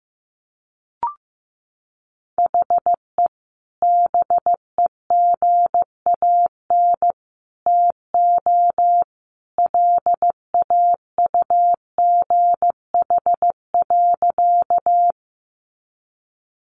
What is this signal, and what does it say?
Morse code, heard as a keyed tone, about 15 wpm, 700 Hz. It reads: HE BEGAN TO LAUGH.
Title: Sentences from Top 500 Words 15wpm